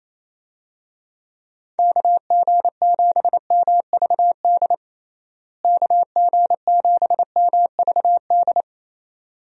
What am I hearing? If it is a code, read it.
Morse code, heard as a keyed tone, about 28 wpm, 700 Hz. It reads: KG7MVD KG7MVD
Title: USA Call Signs - 6 Characters - Repeated 2x - 28wpm